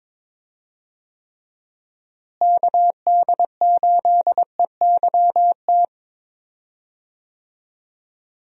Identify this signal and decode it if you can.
Morse code, heard as a keyed tone, about 22 wpm, 700 Hz. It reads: KD8EYT
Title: USA Call Signs - 6 Characters - Repeated 3x - 22wpm